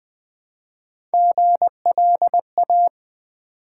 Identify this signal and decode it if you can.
Morse code, heard as a keyed tone, about 20 wpm, 700 Hz. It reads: GLA
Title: Build A Word - 3 Letters At A Time - 6-Letter Words - 20wpm